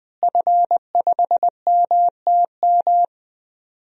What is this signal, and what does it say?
Morse code, heard as a keyed tone, about 20 wpm, 700 Hz. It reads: F5MTM
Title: DXCC-100 Call Signs - 5 Characters - Repeated 3x - 20wpm